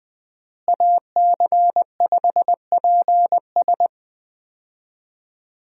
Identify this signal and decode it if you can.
Morse code, heard as a keyed tone, about 20 wpm, 700 Hz. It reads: AC5PS
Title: USA Call Signs - 5 Characters - Repeated 3x - 20wpm